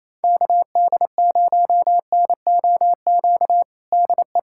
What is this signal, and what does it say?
Morse code, heard as a keyed tone, about 28 wpm, 700 Hz. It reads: KD0NOQ DE
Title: USA Callsign DE Callsign - 6 Characters - 28wpm